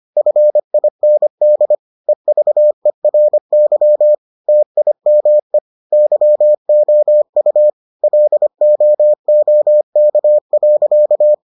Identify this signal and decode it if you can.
Morse code, heard as a keyed tone, about 25 wpm, 600 Hz. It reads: FIND EVERY TIME YOU LOOK.